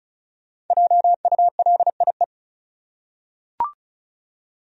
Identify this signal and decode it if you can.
Morse code, heard as a keyed tone, about 34 wpm, 700 Hz. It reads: JULIE
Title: USA Common Names - 35wpm